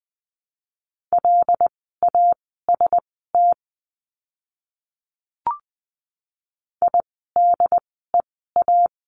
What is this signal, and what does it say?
Morse code, heard as a keyed tone, about 19 wpm, spaced as for 14 wpm, 700 Hz. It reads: LAST IDEA
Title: 108 Lesson 18f - Letter m - Words 20wpm